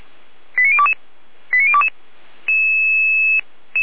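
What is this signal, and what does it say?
Five-tone selective call (ZVEI), heard as quick runs of five tones, tones of about 80 ms. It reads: 89022 89022
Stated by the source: ZVEI five-tone selective call 89011, received on SDR.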